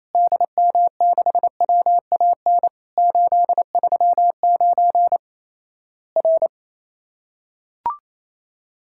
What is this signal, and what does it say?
Morse code, heard as a keyed tone, about 27 wpm, 700 Hz. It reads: DM6WAN 839 R
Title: Worked All Europe DX Contest - QTC Exchange - 28wpm